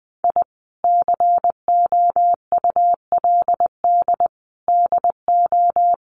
Morse code, heard as a keyed tone, about 20 wpm, 700 Hz. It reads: I COULD DO
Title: Sentences from Top 1300 Words 20wpm